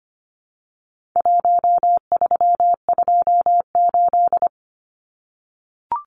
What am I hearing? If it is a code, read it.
Morse code, heard as a keyed tone, about 25 wpm, 700 Hz. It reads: 1328
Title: Numbers - Contest Style - 4-Digit 25wpm